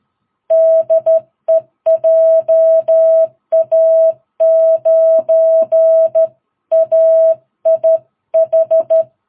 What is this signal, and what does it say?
Morse code, heard as a keyed tone, about 12 wpm, 650 Hz. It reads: DE JA9AIH
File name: jh8cbh / 2022cw